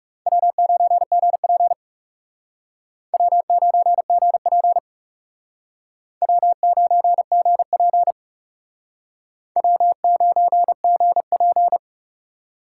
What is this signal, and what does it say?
Morse code, heard as a keyed tone, about 38 wpm, 700 Hz. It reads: W9GP W9GP W9GP W9GP
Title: Speed Racing - USA 1x2 Call Signs 30wpm